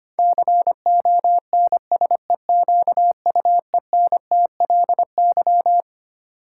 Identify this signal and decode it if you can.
Morse code, heard as a keyed tone, about 25 wpm, 700 Hz. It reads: CONSEQUENTLY
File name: Twelve Letter Words - 25wpm